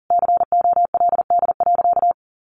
Morse code, heard as a keyed tone, about 40 wpm, 700 Hz. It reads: COLD.
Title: Sentences from Top 1000 Words 40wpm